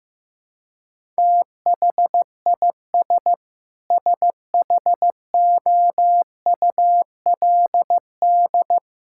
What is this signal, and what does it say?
Morse code, heard as a keyed tone, about 15 wpm, 700 Hz. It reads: THIS SHOULD
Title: Sentences from Top 1000 Words - Review Part 6 - 15wpm